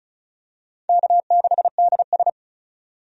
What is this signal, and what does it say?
Morse code, heard as a keyed tone, about 34 wpm, 700 Hz. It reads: K6DS
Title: USA Call Signs - 4 Characters - 35wpm